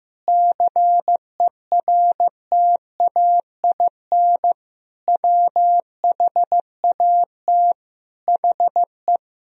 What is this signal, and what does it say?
Morse code, heard as a keyed tone, about 16 wpm, 700 Hz. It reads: CERTAIN WHAT HE
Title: Sentences from Top 1000 Words - Review Part 2 - 15wpm